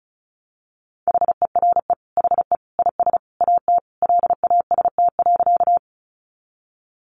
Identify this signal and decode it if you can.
Morse code, heard as a keyed tone, about 34 wpm, 700 Hz. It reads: HERE HE IS AT LAST.
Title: Sentences from Top 300 Words 35wpm